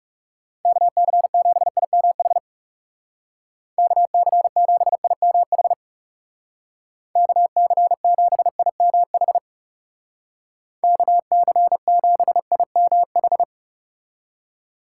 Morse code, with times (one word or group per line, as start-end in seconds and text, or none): 0.7-2.4 KC7IMH
3.8-5.7 KC7IMH
7.2-9.4 KC7IMH
10.8-13.4 KC7IMH